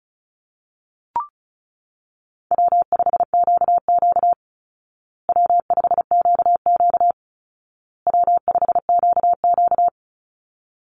Morse code, heard as a keyed tone, about 34 wpm, 700 Hz. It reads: W5QQ W5QQ W5QQ
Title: Easy Call Signs Repeated 3x - 35wpm